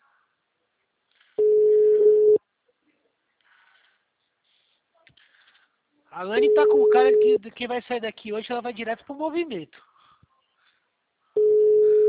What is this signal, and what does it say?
Ringback tone (Europe, 425 Hz), with speech.